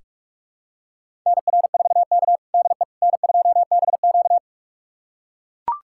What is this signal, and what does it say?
Morse code, heard as a keyed tone, about 44 wpm, 690 Hz. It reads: NR4K DE N2BQ
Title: USA Callsign DE Callsign - 4 Characters - 45wpm